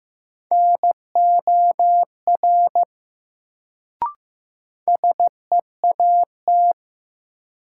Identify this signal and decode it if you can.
Morse code, heard as a keyed tone, about 15 wpm, 700 Hz. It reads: NOR SEAT
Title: Sentences from Top 1500 Words - New Word Drill Alt - 15wpm